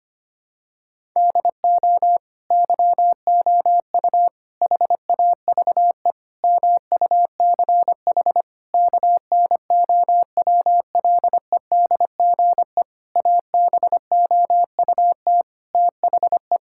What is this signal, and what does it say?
Morse code, heard as a keyed tone, about 25 wpm, 700 Hz. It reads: DO YOU HAVE MUCH KNOWLEDGE ABOUT THE
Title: Sentences from Top 1000 Words - Review Part 6 - 25wpm